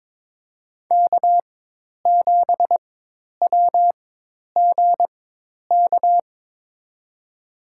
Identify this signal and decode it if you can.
Morse code, heard as a keyed tone, about 23 wpm, spaced as for 10 wpm, 700 Hz. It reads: K7WGK
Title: USA Call Signs - 5 Characters - 4x Character Spacing - 22wpm